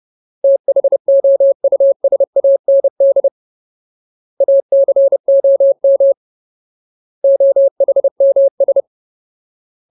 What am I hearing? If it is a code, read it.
Morse code, heard as a keyed tone, about 30 wpm, 550 Hz. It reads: THOUSAND ACOM OHMS
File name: Ultimate Word List / 30WPM Ultimate